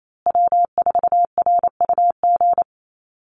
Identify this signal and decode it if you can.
Morse code, heard as a keyed tone, about 28 wpm, 700 Hz. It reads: W4RUG
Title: USA Call Signs Repeated 2x 28wpm